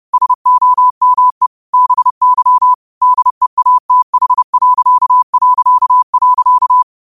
Morse code, heard as a keyed tone, about 30 wpm, 1000 Hz. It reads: SOME BY DEATH...
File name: qotdfiles / 30wpm